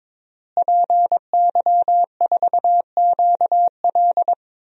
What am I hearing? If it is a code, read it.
Morse code, heard as a keyed tone, about 22 wpm, 700 Hz. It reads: PY4QL
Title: DXCC-100 Call Signs - 5 Characters - Repeated 2x - 22wpm